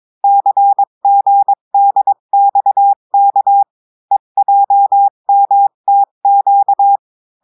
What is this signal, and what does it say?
Morse code, heard as a keyed tone, about 22 wpm, 800 Hz. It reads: CGDXK EJMTQ
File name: cours-du-mercredi / audios / cours-853-du-2-mai-2024-etxte chiffre et texte clair-a-20-wpm